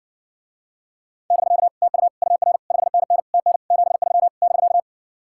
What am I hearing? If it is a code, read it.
Morse code, heard as a keyed tone, about 60 wpm, 700 Hz. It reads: <BK> TU UR 5NN TN 73 <BK>